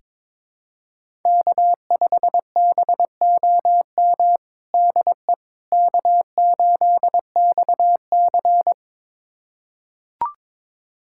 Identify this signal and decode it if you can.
Morse code, heard as a keyed tone, about 22 wpm, 700 Hz. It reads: K5BOM DE K8XC
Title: USA Callsign DE Callsign 22wpm